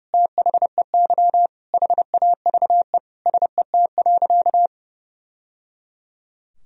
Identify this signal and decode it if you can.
Morse code, heard as a keyed tone, about 30 wpm, 700 Hz. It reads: THEY HAVE SET.
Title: Sentences from Top 1000 Words - Review Part 2 - 30wpm